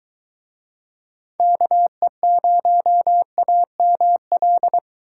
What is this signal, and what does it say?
Morse code, heard as a keyed tone, about 20 wpm, 700 Hz. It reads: KE0AML